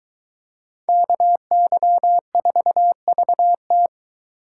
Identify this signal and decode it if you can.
Morse code, heard as a keyed tone, about 23 wpm, 700 Hz. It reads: KY4VT